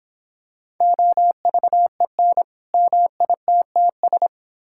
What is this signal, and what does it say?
Morse code, heard as a keyed tone, about 26 wpm, 700 Hz. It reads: OVEN MITTS